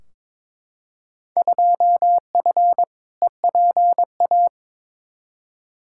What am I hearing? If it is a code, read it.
Morse code, heard as a keyed tone, about 22 wpm, 700 Hz. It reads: 2F EPA
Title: Field Day Exchanges - 22wpm